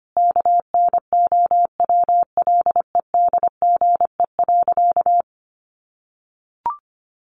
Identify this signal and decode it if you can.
Morse code, heard as a keyed tone, about 25 wpm, 700 Hz. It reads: KNOWLEDGE.
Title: Sentences from Top 1300 Words 25wpm